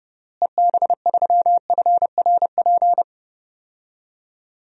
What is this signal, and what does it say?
Morse code, heard as a keyed tone, about 30 wpm, 700 Hz. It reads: EB3FRP
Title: DXCC-100 Call Signs - 6 Characters - Repeated 2x - 30wpm